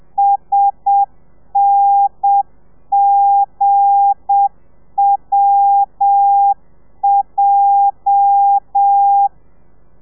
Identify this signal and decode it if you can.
Morse code, heard as a keyed tone, about 7 wpm, 800 Hz. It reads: SNGWJ
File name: telegrafia / i2tfj / Lez8 30